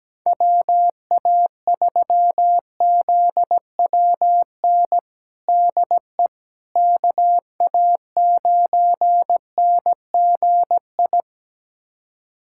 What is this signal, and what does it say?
Morse code, heard as a keyed tone, about 17 wpm, 700 Hz. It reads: WA3ZWN DE KA9NGI
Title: USA Callsign DE Callsign 17wpm